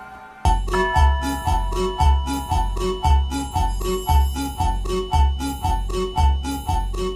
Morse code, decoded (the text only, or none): none